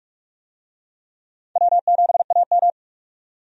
Morse code, heard as a keyed tone, about 46 wpm, 700 Hz. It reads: W7AM
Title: USA Call Signs - 4 Characters - 45wpm